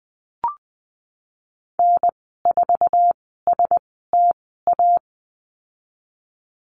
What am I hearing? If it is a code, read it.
Morse code, heard as a keyed tone, about 20 wpm, spaced as for 14 wpm, 700 Hz. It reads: N4STA
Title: 048 Lesson 9g - Number 4 - Call Signs 20wpm